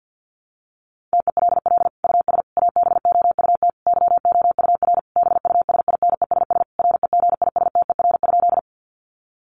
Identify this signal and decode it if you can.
Morse code, heard as a keyed tone, about 50 wpm, 700 Hz. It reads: TELL US ABOUT YOUR BUSINESS REGISTER?
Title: Sentences from Top 1200 Words 50wpm